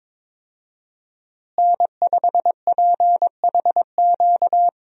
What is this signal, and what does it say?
Morse code, heard as a keyed tone, about 22 wpm, 700 Hz. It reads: N5PHQ
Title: USA Call Signs - 5 Characters - Repeated 2x - 22wpm